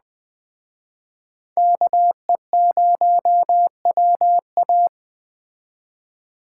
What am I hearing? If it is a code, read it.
Morse code, heard as a keyed tone, about 19 wpm, 700 Hz. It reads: KE0WA